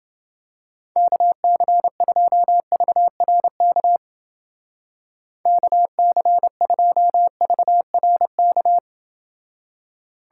Morse code, heard as a keyed tone, about 29 wpm, 700 Hz. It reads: KC2VRK KC2VRK